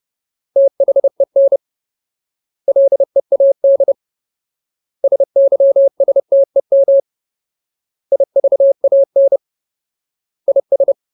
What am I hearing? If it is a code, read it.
Morse code, heard as a keyed tone, about 30 wpm, 550 Hz. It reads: THEN LEAD SYSTEM IVAN IS
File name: Ultimate Word List / 30WPM Ultimate